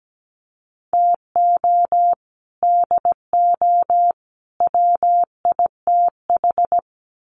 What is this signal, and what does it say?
Morse code, heard as a keyed tone, about 17 wpm, 700 Hz. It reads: TO DO WITH
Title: Sets of 3 Words 17wpm